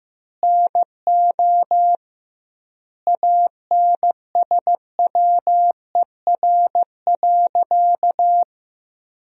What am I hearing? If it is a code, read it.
Morse code, heard as a keyed tone, about 15 wpm, 700 Hz. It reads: NO ANSWER.
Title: 2x Word-Spacing - Sentences from Top 300 Words - 15wpm